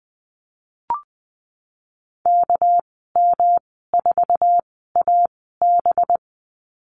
Morse code, heard as a keyed tone, about 20 wpm, spaced as for 15 wpm, 700 Hz. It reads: KM4AB
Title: 207 Lesson 33g - Letter j - Call Signs 20wpm